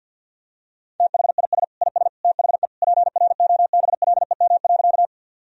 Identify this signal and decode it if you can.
Morse code, heard as a keyed tone, about 51 wpm, 700 Hz. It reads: THIS IS THE PROBLEM.
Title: Sentences from Top 1000 Words - Review Part 2 - 50wpm